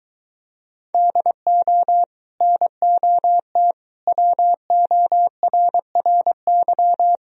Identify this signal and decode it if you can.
Morse code, heard as a keyed tone, about 23 wpm, 700 Hz. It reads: DO NOT WORRY